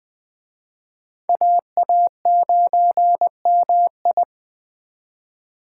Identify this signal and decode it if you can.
Morse code, heard as a keyed tone, about 20 wpm, 700 Hz. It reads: AA9MI